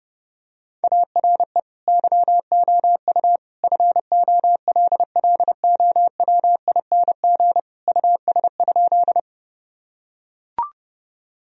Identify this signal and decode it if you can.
Morse code, heard as a keyed tone, about 30 wpm, 700 Hz. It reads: ARE YOU FOLLOWING US?
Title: Sentences from Top 1000 Words - Review Part 4 - 30wpm